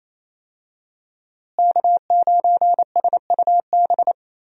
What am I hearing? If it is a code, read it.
Morse code, heard as a keyed tone, about 27 wpm, 700 Hz. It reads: K9SUB
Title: USA Call Signs - 5 Characters - Repeated 2x - 28wpm